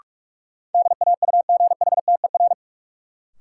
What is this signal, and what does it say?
Morse code, heard as a keyed tone, about 46 wpm, 700 Hz. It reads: DAUGHTER
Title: Sentences from Top 1500 Words - New Word Drill Alt - 45wpm